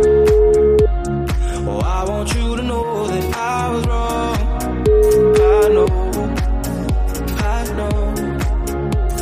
Ringback tone (Europe, 425 Hz), with noise.